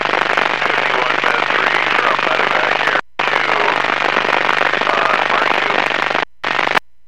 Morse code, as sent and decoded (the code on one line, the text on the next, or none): none